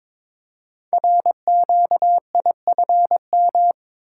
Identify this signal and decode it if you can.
Morse code, heard as a keyed tone, about 22 wpm, 700 Hz. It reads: RQIFM
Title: Code Groups - 5 Letters - 22wpm